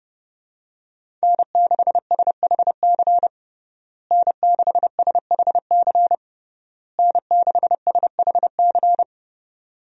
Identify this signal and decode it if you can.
Morse code, heard as a keyed tone, about 32 wpm, 700 Hz. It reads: N6SHC N6SHC N6SHC
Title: USA Call Signs - 5 Characters - Repeated 3x - 30wpm